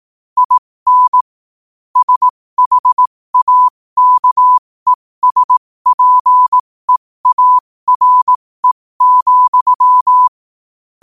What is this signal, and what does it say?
Morse code, heard as a keyed tone, about 18 wpm, spaced as for 15 wpm, 1000 Hz. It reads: IN SHAKESPEARE,
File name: qotdfiles / 15wpm